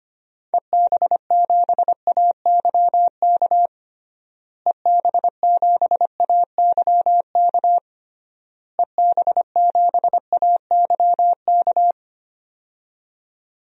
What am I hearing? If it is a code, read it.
Morse code, heard as a keyed tone, about 25 wpm, 700 Hz. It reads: EB7AYK EB7AYK EB7AYK
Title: DXCC-100 Call Signs - 6 Characters - Repeated 3x - 25wpm